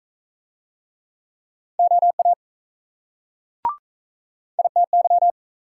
Morse code, heard as a keyed tone, about 39 wpm, 700 Hz. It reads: OA ITY